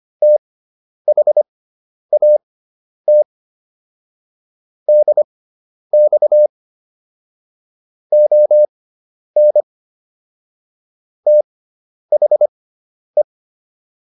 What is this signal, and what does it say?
Morse code, heard as a keyed tone, about 25 wpm, spaced as for 10 wpm, 600 Hz. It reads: THAT DX ON THE